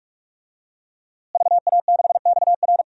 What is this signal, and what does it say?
Morse code, heard as a keyed tone, about 46 wpm, 700 Hz. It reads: VA6XR
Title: DXCC-100 Call Signs - 5 Characters - Repeated 3x - 45wpm